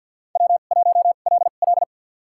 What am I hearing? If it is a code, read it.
Morse code, heard as a keyed tone, about 48 wpm, spaced as for 35 wpm, 700 Hz. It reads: W1LL
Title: USA Call Signs - 4 Characters - 2x Character Spacing - 50wpm